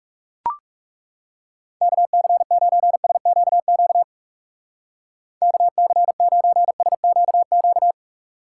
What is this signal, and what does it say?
Morse code, heard as a keyed tone, about 37 wpm, 700 Hz. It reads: KC9SQQ KC9SQQ
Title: Speed Racing - USA 2x3 Call Signs 30wpm